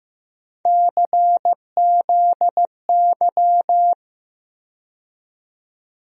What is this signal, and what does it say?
Morse code, heard as a keyed tone, about 15 wpm, 700 Hz. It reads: CZY
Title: USA Call Sign Postfixes - 3 Letters 15wpm